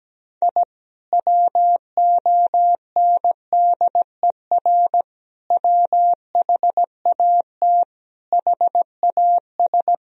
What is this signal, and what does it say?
Morse code, heard as a keyed tone, about 17 wpm, 700 Hz. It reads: I WONDER WHAT HAS
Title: Sentences from Top 1000 Words - Review Part 4 - 17wpm